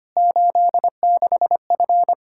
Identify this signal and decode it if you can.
Morse code, heard as a keyed tone, about 25 wpm, 700 Hz. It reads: 86F
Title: K1USN SST QSO Practice 25wpm